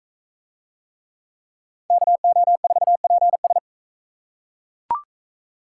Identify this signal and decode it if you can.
Morse code, heard as a keyed tone, about 37 wpm, 700 Hz. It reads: KO4PS